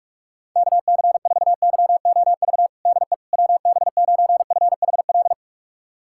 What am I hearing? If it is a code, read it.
Morse code, heard as a keyed tone, about 41 wpm, 700 Hz. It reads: KC4YOV DE WB9FHL
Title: USA Callsign DE Callsign - 6 Characters - 45wpm